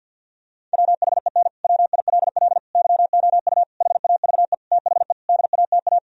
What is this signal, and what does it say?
Morse code, heard as a keyed tone, about 50 wpm, 700 Hz. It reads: WHEN WILL YOU HAVE THE DATA
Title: Sentences from Top 1000 Words - Review Part 3 - 50wpm